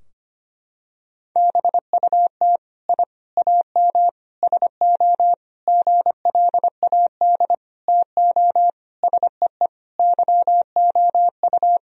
Morse code, heard as a keyed tone, about 25 wpm, 700 Hz. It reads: BUT I AM SO GLAD TO SEE YOU
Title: Sentences from Top 1000 Words - Review Part 2 - 25wpm